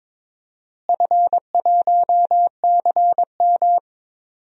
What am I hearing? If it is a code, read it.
Morse code, heard as a keyed tone, about 22 wpm, 700 Hz. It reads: F1CM
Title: DXCC-100 Call Signs - 4 Characters - Repeated 2x - 22wpm